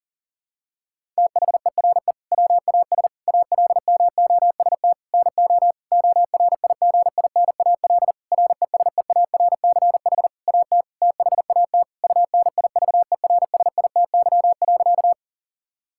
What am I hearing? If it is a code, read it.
Morse code, heard as a keyed tone, about 40 wpm, 700 Hz. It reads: THERE WAS ALMOST NO ORIGINAL RESEARCH AT THAT UNIVERSITY.
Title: Sentences from Top 1000 Words - Review Part 6 - 40wpm